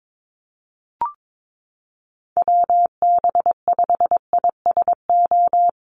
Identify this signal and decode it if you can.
Morse code, heard as a keyed tone, about 22 wpm, 700 Hz. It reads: WB5ISO
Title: USA Call Signs Repeated 3x 22wpm